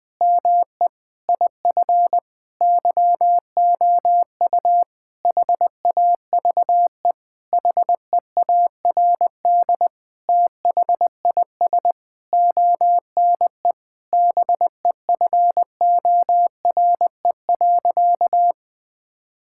Morse code, heard as a keyed tone, about 21 wpm, 700 Hz. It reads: ME IF YOU HAVE HEARD THIS ONE BEFORE.
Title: Sentences with 8-Word Phrases - Spoken Beginning - 20wpm